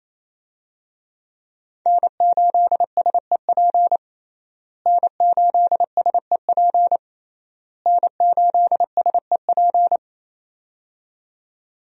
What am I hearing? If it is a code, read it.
Morse code, heard as a keyed tone, about 28 wpm, 700 Hz. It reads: N8SEP N8SEP N8SEP
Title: USA Call Signs - 5 Characters - Repeated 3x - 28wpm